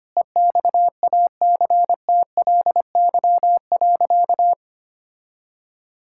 Morse code, heard as a keyed tone, about 25 wpm, 700 Hz. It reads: EXACTLY.
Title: Sentences from Top 1000 Words - Review Part 6 - 25wpm